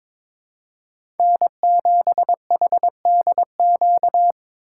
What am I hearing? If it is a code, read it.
Morse code, heard as a keyed tone, about 22 wpm, 700 Hz. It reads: N7HDQ
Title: USA Call Signs - 5 Characters - 2x Character Spacing - 22wpm